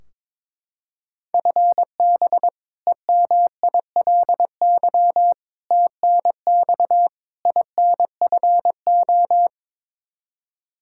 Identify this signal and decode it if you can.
Morse code, heard as a keyed tone, about 22 wpm, 700 Hz. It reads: FB EMILY TNX INFO
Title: Practice QSO - Exchange 3 - More Info - 22wpm